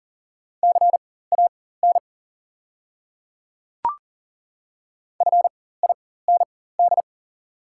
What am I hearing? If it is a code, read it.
Morse code, heard as a keyed tone, about 40 wpm, spaced as for 19 wpm, 690 Hz. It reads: CAN FIND
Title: 4x Character Spacing - Top 100 Words - 40wpm